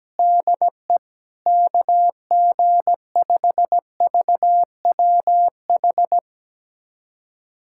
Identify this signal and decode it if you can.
Morse code, heard as a keyed tone, about 17 wpm, 700 Hz. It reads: DE KG5VWH
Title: USA Callsign DE Callsign 17wpm